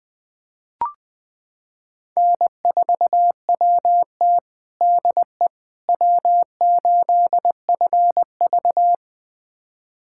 Morse code, heard as a keyed tone, about 20 wpm, 700 Hz. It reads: N4WT DE W8FV
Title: USA Callsign DE Callsign - 4 Characters - 20wpm